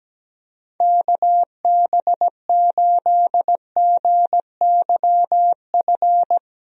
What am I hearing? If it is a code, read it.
Morse code, heard as a keyed tone, about 17 wpm, 700 Hz. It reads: KB8GYF
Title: USA Call Signs - 6 Characters - Repeated 3x - 17wpm